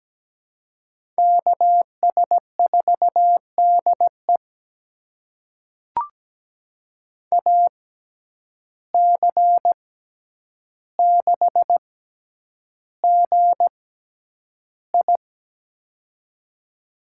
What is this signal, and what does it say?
Morse code, heard as a keyed tone, about 17 wpm, spaced as for 9 wpm, 700 Hz. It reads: KS4DE AC6GI
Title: USA Call Signs - 5 Characters - 6x Character Spacing - 17wpm